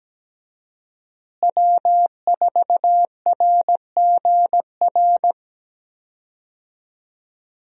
Morse code, heard as a keyed tone, about 17 wpm, 700 Hz. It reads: W4RGR
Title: USA Call Signs - 5 Characters - Repeated 3x - 17wpm